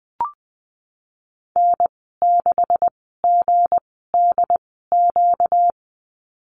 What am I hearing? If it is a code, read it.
Morse code, heard as a keyed tone, about 20 wpm, spaced as for 14 wpm, 700 Hz. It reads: N6GDQ
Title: 245 Lesson 39g - Letter z - Call Signs 20wpm